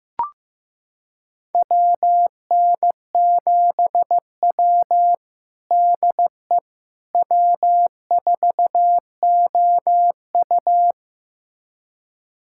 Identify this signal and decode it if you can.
Morse code, heard as a keyed tone, about 15 wpm, 700 Hz. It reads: WN7W DE W4OU
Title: USA Callsign DE Callsign - 4 Characters - 15wpm